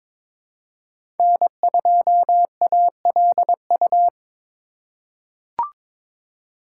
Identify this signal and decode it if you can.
Morse code, heard as a keyed tone, about 22 wpm, 700 Hz. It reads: N2ALU
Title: USA Call Signs - 3x Character Spacing - 22wpm